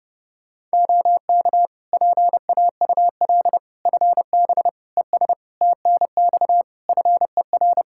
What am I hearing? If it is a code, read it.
Morse code, heard as a keyed tone, about 30 wpm, 700 Hz. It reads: OK PAUL FB ES TNX FER
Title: K1USN SST QSO Practice 30wpm